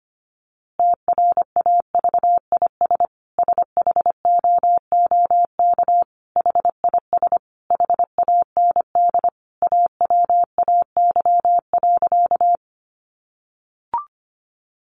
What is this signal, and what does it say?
Morse code, heard as a keyed tone, about 25 wpm, 700 Hz. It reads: TRAVIS SHOOK HIS HAND AWAY.
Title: A Sound Of Thunder 25wpm